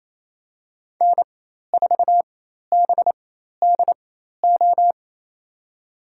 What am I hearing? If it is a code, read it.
Morse code, heard as a keyed tone, about 28 wpm, spaced as for 13 wpm, 700 Hz. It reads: N4BDO
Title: USA Call Signs - 4x Character Spacing - 28wpm